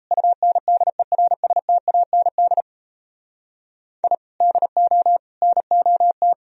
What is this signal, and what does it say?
Morse code, heard as a keyed tone, about 35 wpm, 700 Hz. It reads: UNDERSTAND I DO NOT